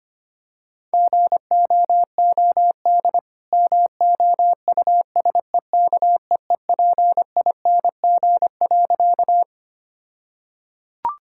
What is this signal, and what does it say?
Morse code, heard as a keyed tone, about 25 wpm, 700 Hz. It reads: GOOD MOUSEKEEPING.
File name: practice / 25wpm / All-Age Appropriate Jokes - 25wpm